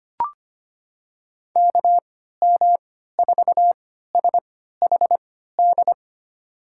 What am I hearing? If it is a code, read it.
Morse code, heard as a keyed tone, about 25 wpm, spaced as for 14 wpm, 700 Hz. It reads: KM4SHD
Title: USA Call Signs - 3x Character Spacing - 25wpm